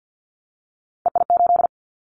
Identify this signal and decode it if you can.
Morse code, heard as a keyed tone, about 50 wpm, 700 Hz. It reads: EI8
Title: Callsign Prefixes for Top 50 DXCC - 50wpm